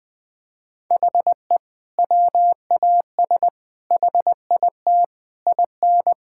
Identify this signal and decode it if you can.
Morse code, heard as a keyed tone, about 20 wpm, 700 Hz. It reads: HE WAS HIT IN
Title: Sentences with 6-Word Phrases - 20wpm